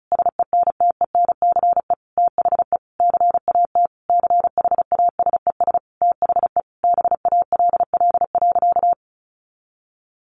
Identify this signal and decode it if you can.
Morse code, heard as a keyed tone, about 34 wpm, 700 Hz. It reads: SENTENCE THE CAT CHASES THE BALL.
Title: Additional Easy Phrases 35wpm